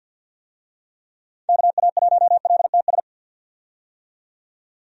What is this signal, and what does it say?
Morse code, heard as a keyed tone, about 47 wpm, 700 Hz. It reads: KA1LTS
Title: USA Call Signs - 6 Characters - 50wpm